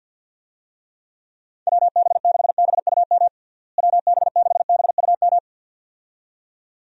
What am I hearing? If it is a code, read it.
Morse code, heard as a keyed tone, about 50 wpm, 700 Hz. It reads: WB6BUM WB6BUM
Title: USA Call Signs - 6 Characters - Repeated 2x - 50wpm